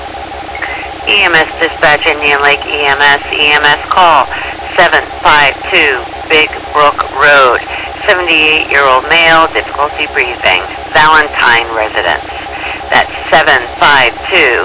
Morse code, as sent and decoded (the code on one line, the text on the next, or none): none